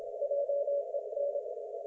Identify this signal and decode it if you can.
Morse code, heard as a keyed tone, about 27 wpm, 570 Hz. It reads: JEN
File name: Archiv / CUC216 / ok1fhi cul216